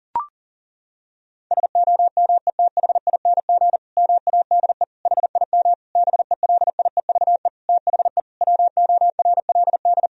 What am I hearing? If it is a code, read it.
Morse code, heard as a keyed tone, about 40 wpm, 700 Hz. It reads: SOMETHING MADE HIM BELIEVE THE WORLD
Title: Sentences from Top 1000 Words - Review Part 1 - 40wpm